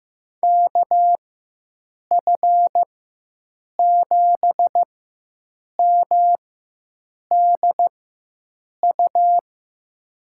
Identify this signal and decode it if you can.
Morse code, heard as a keyed tone, about 15 wpm, spaced as for 7 wpm, 700 Hz. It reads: KF7MDU
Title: USA Call Signs - 6 Characters - 4x Character Spacing - 15wpm